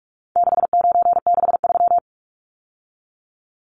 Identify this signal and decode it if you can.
Morse code, heard as a keyed tone, about 44 wpm, 700 Hz. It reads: 6963
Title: Numbers - 4-Digit 45wpm